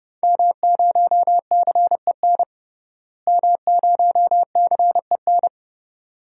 Morse code, heard as a keyed tone, about 32 wpm, 700 Hz. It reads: M0CEN M0CEN
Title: DXCC-100 Call Signs - 5 Characters - Repeated 2x - 30wpm